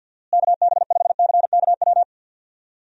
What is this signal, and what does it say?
Morse code, heard as a keyed tone, about 53 wpm, 700 Hz. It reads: KB5CKW
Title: USA Call Signs - 6 Characters - Repeated 2x - 50wpm